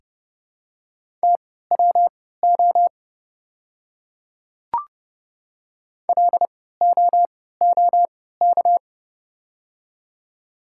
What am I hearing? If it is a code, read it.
Morse code, heard as a keyed tone, about 30 wpm, spaced as for 17 wpm, 700 Hz. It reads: TWO LOOK
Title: 3x Character Spacing - Top 200 Words - 30wpm